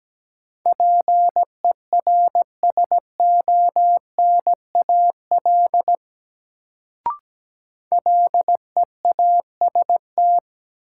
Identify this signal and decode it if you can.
Morse code, heard as a keyed tone, about 17 wpm, 700 Hz. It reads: PERSONAL LEAST
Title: Sentences from Top 700 Words - New Word Drill - 17wpm